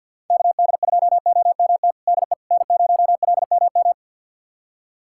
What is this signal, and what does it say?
Morse code, heard as a keyed tone, about 46 wpm, 700 Hz. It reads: KD2OMT DE N0LMM
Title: USA Callsign DE Callsign 50wpm